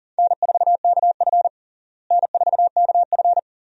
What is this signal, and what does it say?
Morse code, heard as a keyed tone, about 40 wpm, 700 Hz. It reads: N4KF N4KF
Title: USA Call Signs - 4 Characters - Repeated 2x - 40wpm